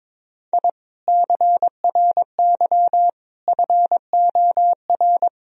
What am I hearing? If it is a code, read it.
Morse code, heard as a keyed tone, about 22 wpm, 700 Hz. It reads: I CRY FOR
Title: Sentences from Top 1000 Words - Review Part 2 - 22wpm